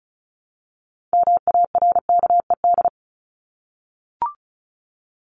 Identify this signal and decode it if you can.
Morse code, heard as a keyed tone, about 34 wpm, 700 Hz. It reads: MARKED
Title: ED Suffix - Top 100 Words with Base Words 35wpm